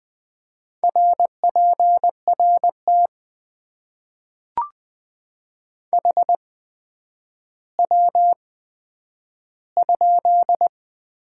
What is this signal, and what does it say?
Morse code, heard as a keyed tone, about 20 wpm, spaced as for 10 wpm, 700 Hz. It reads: RPRT HW?
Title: Top 90 QSO Elements - 8x Character Spacing - 20wpm